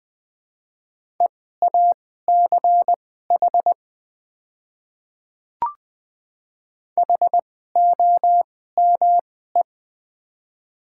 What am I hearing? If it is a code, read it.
Morse code, heard as a keyed tone, about 20 wpm, spaced as for 15 wpm, 700 Hz. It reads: EACH HOME
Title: 2x Character Spacing - Top 500 Words - 20wpm